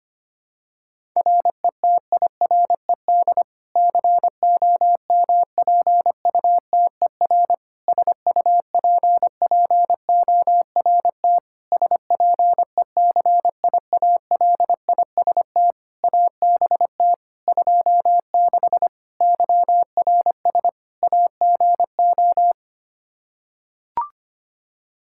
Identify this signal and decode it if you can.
Morse code, heard as a keyed tone, about 25 wpm, 700 Hz. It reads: RETIRED COMPUTER SUPPORT SPECIALIST ABT 26 YRS AGO
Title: Practice QSO - Exchange 3 - More Info - 25wpm